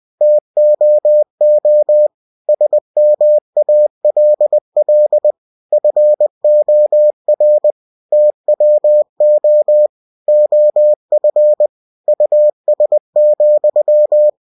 Morse code, heard as a keyed tone, about 20 wpm, 600 Hz. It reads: TOO SMALL FOR TWO OF US,
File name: cw-books / winnie-the-pooh / winnie-20wpm-page-0099